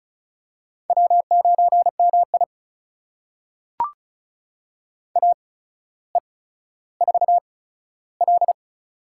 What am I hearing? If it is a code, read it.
Morse code, heard as a keyed tone, about 36 wpm, spaced as for 15 wpm, 700 Hz. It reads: W9MI AE4L
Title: USA Call Signs - 4 Characters - 8x Character Spacing - 35wpm